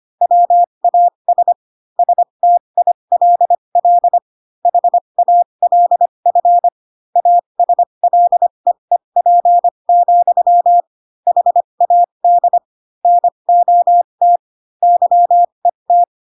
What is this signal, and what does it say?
Morse code, heard as a keyed tone, about 25 wpm, spaced as for 22 wpm, 700 Hz. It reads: WAS STILL HALF ASLEEP, HAD NOT YET
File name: Pinocchio / Pinocchio ch7 18 25 ik1hge